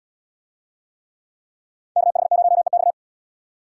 Morse code, heard as a keyed tone, about 71 wpm, 700 Hz. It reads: KH0EX